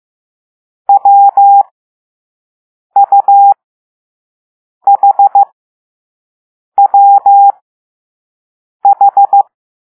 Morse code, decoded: WUHWH